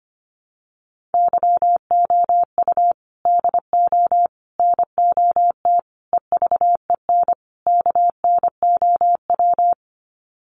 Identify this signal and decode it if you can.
Morse code, heard as a keyed tone, about 25 wpm, 700 Hz. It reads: YOU DO NOT EVEN KNOW
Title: Sets of 5 Words Encore2 25wpm